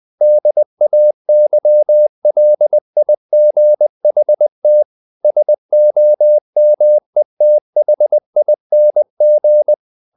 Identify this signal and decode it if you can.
Morse code, heard as a keyed tone, about 20 wpm, 600 Hz. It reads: DAYLIGHT SOMETHING
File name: the old man and the sea / the old man and the sea part 1 20wpm